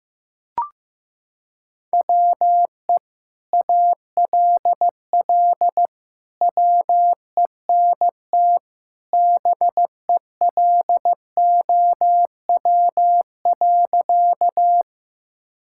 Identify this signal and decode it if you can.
Morse code, heard as a keyed tone, about 16 wpm, 700 Hz. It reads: WE ALL WENT BELOW.
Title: Sentences from Top 1000 Words - Review Part 5 - 15wpm